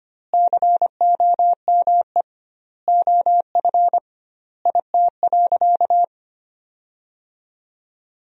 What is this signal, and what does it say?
Morse code, heard as a keyed tone, about 25 wpm, 700 Hz. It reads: COME OF IT.
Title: 2x Word-Spacing - Sentences from Top 500 Words - 25wpm